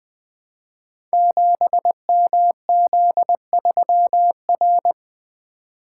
Morse code, heard as a keyed tone, about 20 wpm, 700 Hz. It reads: 7MZ3R
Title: Code Groups - 5 Letter:Numbers - 20wpm